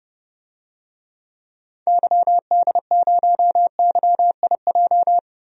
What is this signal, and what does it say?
Morse code, heard as a keyed tone, about 30 wpm, 700 Hz. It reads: YD0YIJ
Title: DXCC-100 Call Signs - 6 Characters - Repeated 3x - 30wpm